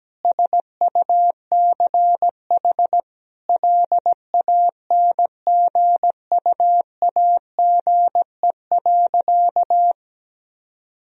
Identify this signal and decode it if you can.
Morse code, heard as a keyed tone, about 17 wpm, 700 Hz. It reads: SUCH LANGUAGE.
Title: Sentences from Top 1000 Words - Review Part 3 - 17wpm